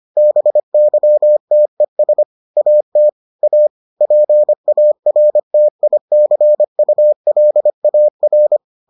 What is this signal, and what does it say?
Morse code, heard as a keyed tone, about 25 wpm, 600 Hz. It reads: BYTES AT A PARTICULAR